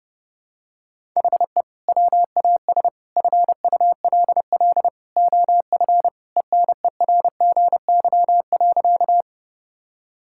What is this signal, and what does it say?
Morse code, heard as a keyed tone, about 32 wpm, 700 Hz. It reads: HE WAS FULL OF ENERGY.
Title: Sentences from Top 1000 Words - Review Part 6 - 30wpm